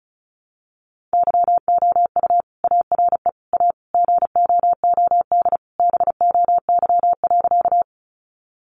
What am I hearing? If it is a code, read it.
Morse code, heard as a keyed tone, about 34 wpm, 700 Hz. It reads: YOU ARE A GOOD BOY.
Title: Sentences from Top 200 Words 35wpm